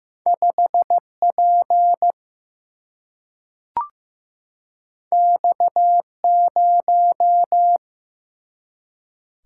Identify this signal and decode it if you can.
Morse code, heard as a keyed tone, about 15 wpm, 700 Hz. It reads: 5 P X 0
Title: Single-Letter Single-Number 15wpm